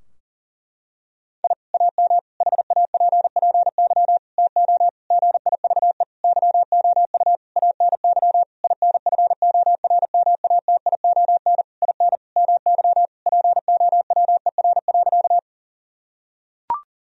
Morse code, heard as a keyed tone, about 40 wpm, 700 Hz. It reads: I AM HAPPY TO GIVE YOU ANY INFORMATION IN MY POWER.
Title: Sentences from Top 1000 Words - Review Part 3 - 40wpm